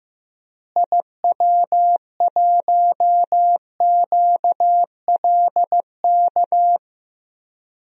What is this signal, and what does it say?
Morse code, heard as a keyed tone, about 15 wpm, 700 Hz. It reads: IW1QLK
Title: DXCC-100 Call Signs - 6 Characters - Repeated 2x - 15wpm